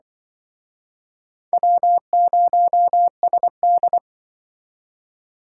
Morse code, heard as a keyed tone, about 24 wpm, 700 Hz. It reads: W0SD
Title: Speed Racing - USA 1x2 Call Signs 20wpm